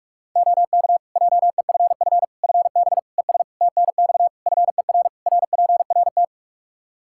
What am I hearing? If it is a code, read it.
Morse code, heard as a keyed tone, about 41 wpm, 700 Hz. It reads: OK JEFF FB ES TNX FER RPRT
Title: K1USN SST QSO Practice 45wpm